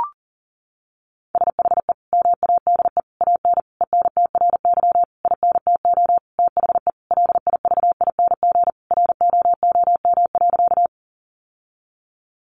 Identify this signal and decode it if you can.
Morse code, heard as a keyed tone, about 40 wpm, 700 Hz. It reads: SHE MADE AN ENTRY INTO THE LIVING ROOM.
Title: Sentences from Top 1500 Words 40wpm